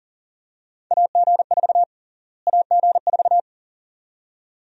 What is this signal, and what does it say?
Morse code, heard as a keyed tone, about 40 wpm, 700 Hz. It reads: AG4 AG4
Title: USA Call Sign Prefixes - Repeated 2x - 40wpm